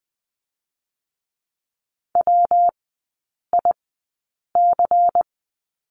Morse code, heard as a keyed tone, about 20 wpm, 700 Hz. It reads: W I C
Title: 114 Lesson 19e - Letter w - Three Characters 20wpm